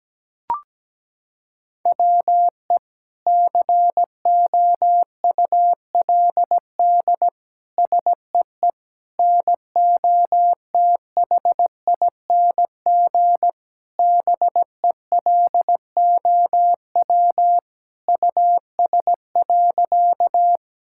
Morse code, heard as a keyed tone, about 17 wpm, 700 Hz. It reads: WE COULD SEE NOTHING BELOW US.
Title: Sentences from Top 1000 Words - Review Part 5 - 17wpm